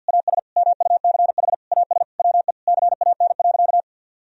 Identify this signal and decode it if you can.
Morse code, heard as a keyed tone, about 50 wpm, 700 Hz. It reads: AS MUCH AS WE CAN.
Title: Sentences from Top 1000 Words - Review Part 3 - 50wpm